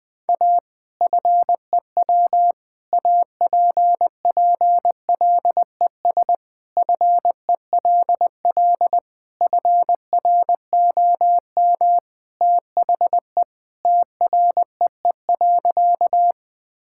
Morse code, heard as a keyed tone, about 20 wpm, 700 Hz. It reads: A FEW APPLES FELL FROM THE TREE.
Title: Sentences with 2-Word Phrases - Spoken Beginning - 20wpm